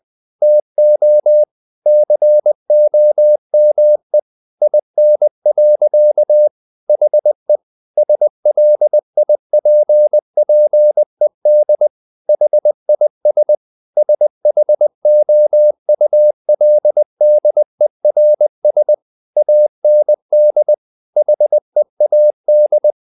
Morse code, heard as a keyed tone, about 20 wpm, 600 Hz. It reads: TO COME IN. HE SLIPPED HIS SHOULDERS AND HEAD
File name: the old man and the sea part 2 20wpm